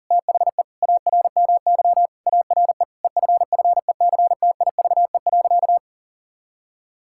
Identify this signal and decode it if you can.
Morse code, heard as a keyed tone, about 40 wpm, 700 Hz. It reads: THE ARMY ARE EFFECTIVE.
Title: Sentences from Top 1000 Words - Review Part 6 - 40wpm